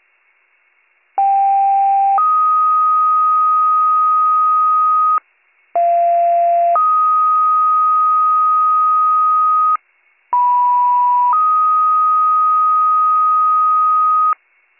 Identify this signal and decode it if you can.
Two-tone page, heard as a one-second tone, then a three-second tone, about 790 then 1250 Hz, about 690 then 1250 Hz, about 980 then 1250 Hz. It is heard from 1.2 to 14.3 s.